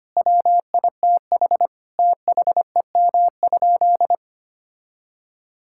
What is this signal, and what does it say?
Morse code, heard as a keyed tone, about 25 wpm, 700 Hz. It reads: WITH THEM?
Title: Sentences from Top 1000 Words - Review Part 5 - 25wpm